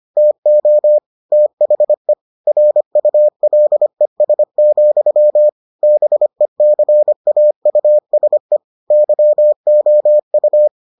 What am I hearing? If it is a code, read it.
Morse code, heard as a keyed tone, about 25 wpm, 600 Hz. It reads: TO THE RULES, BECAUSE YOU